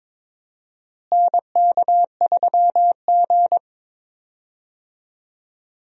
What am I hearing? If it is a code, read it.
Morse code, heard as a keyed tone, about 22 wpm, 700 Hz. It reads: NK3G
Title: USA Call Signs - 4 Characters - Repeated 2x - 22wpm